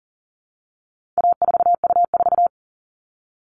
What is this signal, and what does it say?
Morse code, heard as a keyed tone, about 40 wpm, 700 Hz. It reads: A4U4
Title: All Cut Numbers - 4-Digit 40wpm